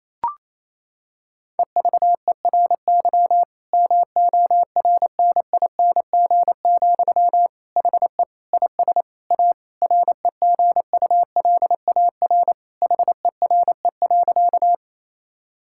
Morse code, heard as a keyed tone, about 27 wpm, 690 Hz. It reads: EVERY MORNING, HE IS A REGULAR HERE.
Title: Sentences from Top 1000 Words - Review Part 6 - 28wpm